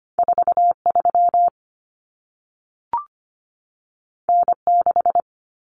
Morse code, heard as a keyed tone, about 25 wpm, 700 Hz. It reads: 43 N6
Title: Common Cut Numbers - 2-Digit 25wpm